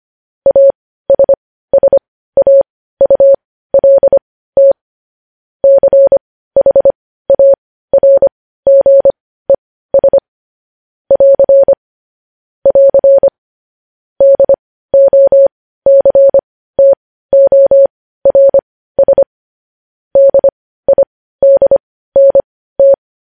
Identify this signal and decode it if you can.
Morse code, heard as a keyed tone, about 25 wpm, spaced as for 15 wpm, 550 Hz. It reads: ASSAULT CHARGES <AR> <AR> DOCTORS DIDNT
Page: News headlines in Morse code at 15 wpm